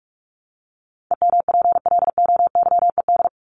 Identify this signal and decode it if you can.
Morse code, heard as a keyed tone, about 44 wpm, 700 Hz. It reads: EMPLOYED
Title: ED Suffix - Top 100 Words with Base Words 45wpm